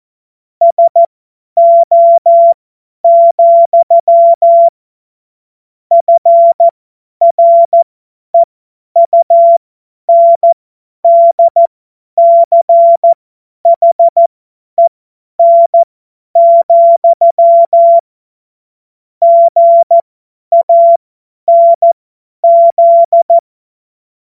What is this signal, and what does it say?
Morse code, heard as a keyed tone, about 14 wpm, spaced as for 10 wpm, 680 Hz. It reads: SO, FREUNDCHEN, GANZ
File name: Morsekurs / Witze / Witz 01. Februar 2022 50 F70